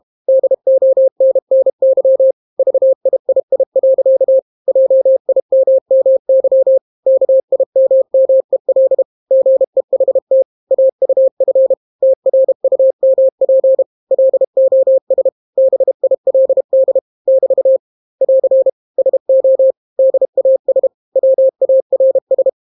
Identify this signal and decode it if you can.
Morse code, heard as a keyed tone, about 32 wpm, 530 Hz. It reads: DONNY VIII. JIMMY KIMMEL GEHT AUF TRUMP LOS BILD <BT> <AR> SO DAS WARS